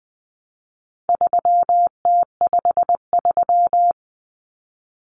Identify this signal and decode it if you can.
Morse code, heard as a keyed tone, about 20 wpm, 700 Hz. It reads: 3T53
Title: Common Cut Numbers - 4-Digit 20wpm